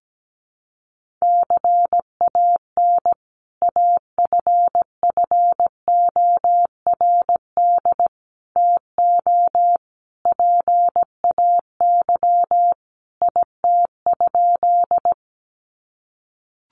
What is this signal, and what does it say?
Morse code, heard as a keyed tone, about 17 wpm, 700 Hz. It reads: CAN AFFORD TO PAY IT?
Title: Sentences from Top 1100 Words 17wpm